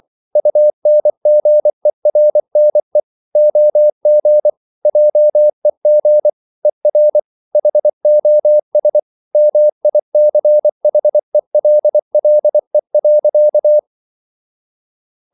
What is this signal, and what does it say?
Morse code, heard as a keyed tone, about 24 wpm, 600 Hz. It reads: UNGERNE OG JEG ER HOS MICHELLE.
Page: Ferie 24 wpm